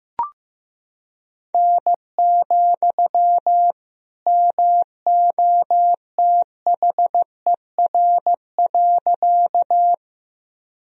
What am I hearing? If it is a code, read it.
Morse code, heard as a keyed tone, about 15 wpm, 700 Hz. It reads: N, MOTHER.
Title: Sentences from Top 200 Words and Abbreviations 15wpm